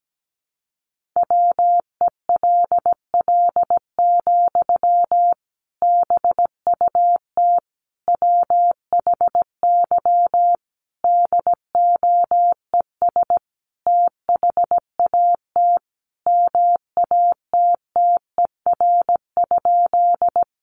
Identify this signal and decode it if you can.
Morse code, heard as a keyed tone, about 17 wpm, 700 Hz. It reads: WELL, BUT WHY DOES THAT MATTER?
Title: Sentences from Top 700 Words 17wpm